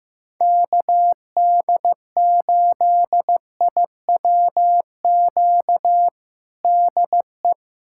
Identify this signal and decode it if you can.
Morse code, heard as a keyed tone, about 15 wpm, 700 Hz. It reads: KD8IWQ DE
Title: USA Callsign DE Callsign - 6 Characters - 15wpm